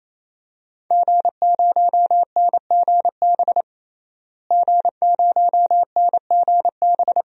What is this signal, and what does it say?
Morse code, heard as a keyed tone, about 27 wpm, 700 Hz. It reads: G0NGB G0NGB
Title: DXCC-100 Call Signs - 5 Characters - Repeated 3x - 28wpm